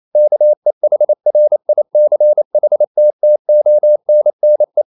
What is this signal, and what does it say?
Morse code, heard as a keyed tone, about 28 wpm, 600 Hz. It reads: KEHRICHTTONNE